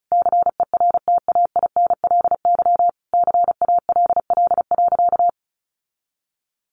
Morse code, heard as a keyed tone, about 34 wpm, 700 Hz. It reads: CERTAINLY CALL.
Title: Sentences from Top 900 Words 35wpm